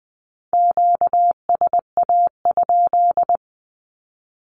Morse code, heard as a keyed tone, about 20 wpm, 700 Hz. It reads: QSA?
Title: Q-Codes with Spoken Meaning - 20wpm